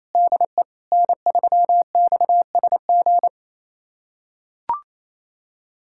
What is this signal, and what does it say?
Morse code, heard as a keyed tone, about 27 wpm, 700 Hz. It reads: DE N3XSG
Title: USA Callsign DE Callsign - 5 Characters - 28wpm